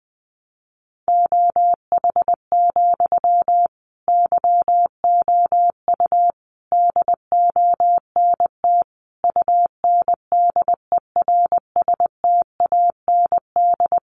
Morse code, heard as a keyed tone, about 20 wpm, 700 Hz. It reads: OH, YOU DONT UNDERSTAND
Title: Sentences from Top 1000 Words 20wpm